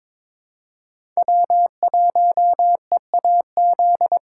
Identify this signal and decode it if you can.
Morse code, heard as a keyed tone, about 22 wpm, 700 Hz. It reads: W1EAZ
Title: USA Callsign DE Callsign 22wpm